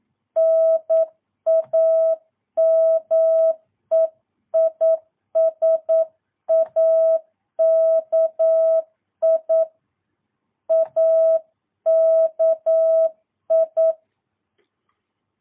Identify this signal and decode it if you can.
Morse code, heard as a keyed tone, about 9 wpm, 650 Hz. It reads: NAME IS AKI AKI
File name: jh8cbh / 2022cw